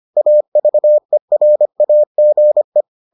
Morse code, heard as a keyed tone, about 25 wpm, 600 Hz. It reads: AVERAGE